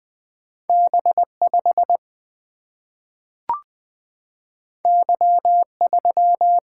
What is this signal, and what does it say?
Morse code, heard as a keyed tone, about 20 wpm, 700 Hz. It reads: B5 Y3
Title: Single-Letter Single-Number 20wpm